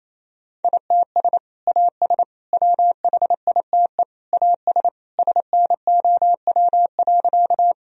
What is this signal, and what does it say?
Morse code, heard as a keyed tone, about 27 wpm, 700 Hz. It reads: ITS AS WHITE AS SNOW.
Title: Sentences from Top 1000 Words - Review Part 3 - 28wpm